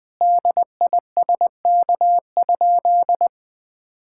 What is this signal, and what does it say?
Morse code, heard as a keyed tone, about 20 wpm, 700 Hz. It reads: DISK?
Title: Sentences from Top 1000 Words - Review Part 6 - 20wpm